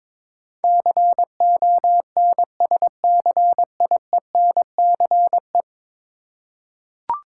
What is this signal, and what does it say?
Morse code, heard as a keyed tone, about 21 wpm, 700 Hz. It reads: CONSCIENCE
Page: morse-code-ninja-practice-22wpm